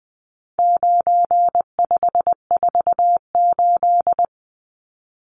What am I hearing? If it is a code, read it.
Morse code, heard as a keyed tone, about 20 wpm, 700 Hz. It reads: 9548
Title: Numbers - 4-Digit 20wpm